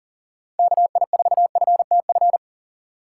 Morse code, heard as a keyed tone, about 37 wpm, 700 Hz. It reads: KI4FTF
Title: USA Call Signs - 2x Character Spacing - 40wpm